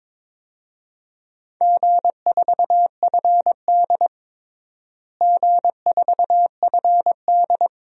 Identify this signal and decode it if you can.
Morse code, heard as a keyed tone, about 22 wpm, 700 Hz. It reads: G4FD G4FD
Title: DXCC-100 Call Signs - 4 Characters - Repeated 2x - 22wpm